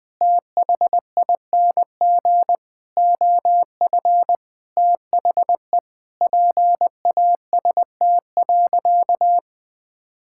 Morse code, heard as a keyed tone, about 20 wpm, 700 Hz. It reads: THING OF THE PAST.
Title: Sentences with 6-Word Phrases - 20wpm